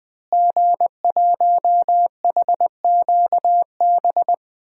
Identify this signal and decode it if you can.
Morse code, heard as a keyed tone, about 20 wpm, 700 Hz. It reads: G1HQB
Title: DXCC-100 Call Signs - 5 Characters - Repeated 3x - 20wpm